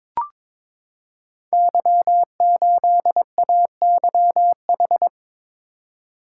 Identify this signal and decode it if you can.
Morse code, heard as a keyed tone, about 22 wpm, 690 Hz. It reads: Y8AYH
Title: Code Groups - 5 Letter:Numbers - 22wpm